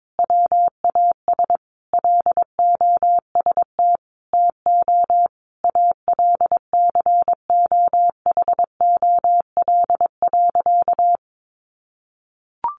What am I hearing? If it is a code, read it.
Morse code, heard as a keyed tone, about 22 wpm, 700 Hz. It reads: WAS LOST TO ALCOHOL.
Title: Sentences from Top 1400 Words 22wpm